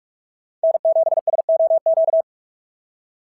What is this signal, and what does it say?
Morse code, heard as a keyed tone, about 43 wpm, 690 Hz. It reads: N7SOQ
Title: SOTA QSO Practice 45wpm